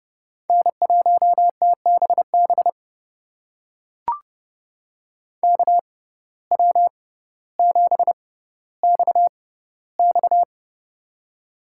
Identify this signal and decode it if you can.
Morse code, heard as a keyed tone, about 31 wpm, spaced as for 15 wpm, 700 Hz. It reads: N1TBB KW7XX
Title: USA Call Signs - 5 Characters - 6x Character Spacing - 30wpm